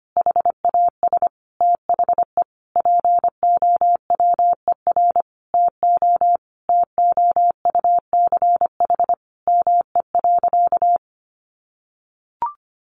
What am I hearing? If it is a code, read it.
Morse code, heard as a keyed tone, about 25 wpm, 700 Hz. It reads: HAS THE POWER TO TOUCH ME.
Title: Sentences from Top 1200 Words 25wpm